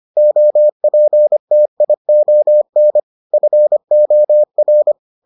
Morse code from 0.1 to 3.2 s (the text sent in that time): OPTION